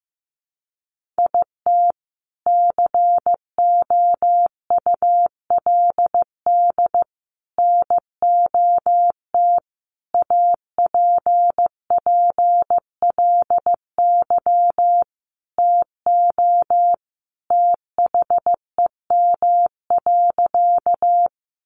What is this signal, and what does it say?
Morse code, heard as a keyed tone, about 15 wpm, 700 Hz. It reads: IT COULD NOT APPLY TO THEM.
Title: Sentences from Top 1200 Words 15wpm